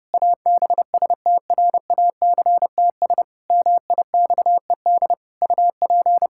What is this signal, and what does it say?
Morse code, heard as a keyed tone, about 30 wpm, 700 Hz. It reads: ABSTRACTS MIXED UP
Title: Pangram Challenge v2 - 30wpm